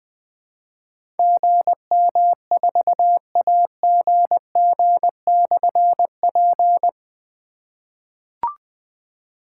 Morse code, heard as a keyed tone, about 20 wpm, 700 Hz. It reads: GM4AGG/P
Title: Difficult Call Signs 20wpm